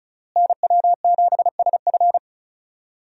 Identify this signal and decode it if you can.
Morse code, heard as a keyed tone, about 36 wpm, 700 Hz. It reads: NW7SF
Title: USA Call Signs - 5 Characters - Repeated 3x - 35wpm